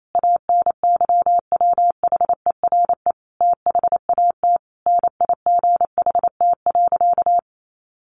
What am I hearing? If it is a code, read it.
Morse code, heard as a keyed tone, about 28 wpm, 700 Hz. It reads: ANYWHERE THAT NIGHT.
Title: Sentences from Top 1400 Words 28wpm